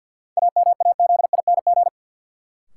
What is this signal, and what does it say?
Morse code, heard as a keyed tone, about 49 wpm, 700 Hz. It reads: AMAZING
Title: Sentences from Top 1500 Words - New Word Drill - 50wpm